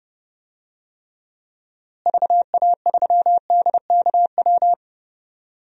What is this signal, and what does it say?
Morse code, heard as a keyed tone, about 32 wpm, 700 Hz. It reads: VA3DKW
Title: DXCC-100 Call Signs - 6 Characters - Repeated 3x - 30wpm